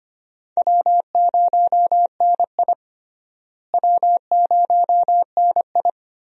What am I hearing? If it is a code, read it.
Morse code, heard as a keyed tone, about 25 wpm, 700 Hz. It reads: W0NI W0NI
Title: USA Call Signs - 4 Characters - Repeated 2x - 25wpm